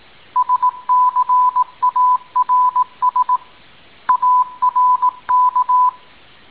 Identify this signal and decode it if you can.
Morse code, heard as a keyed tone, about 18 wpm, 1000 Hz. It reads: SCARS ARK
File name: tin-upgrade / QSO 2018 05 17